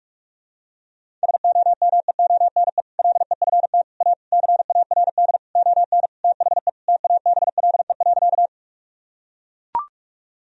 Morse code, heard as a keyed tone, about 43 wpm, 700 Hz. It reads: SOMEONE LEFT A CARD ON THE TABLE.
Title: Sentences from Top 1000 Words - Review Part 4 - 45wpm